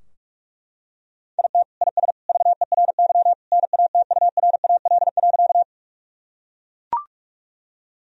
Morse code, heard as a keyed tone, about 41 wpm, 700 Hz. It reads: IT IS VERY NATURAL.
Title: Sentences from Top 1000 Words - Review Part 4 - 45wpm